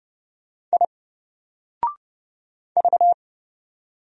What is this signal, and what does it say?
Morse code, heard as a keyed tone, about 31 wpm, 700 Hz. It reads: I V
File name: Single-Letter-Number-30wpm